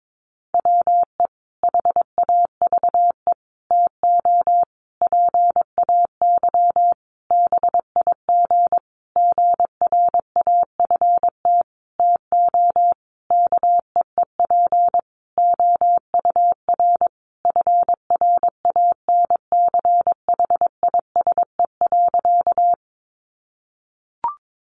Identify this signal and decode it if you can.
Morse code, heard as a keyed tone, about 22 wpm, 700 Hz. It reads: WE HAVE TO PAY BIG GRAFT TO KEEP OUR FRANCHISE.
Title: A Sound Of Thunder 22wpm